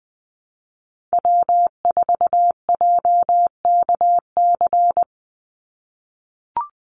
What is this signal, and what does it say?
Morse code, heard as a keyed tone, about 20 wpm, 700 Hz. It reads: W4JKC
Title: NA SOTA Chaser Callsigns 20wpm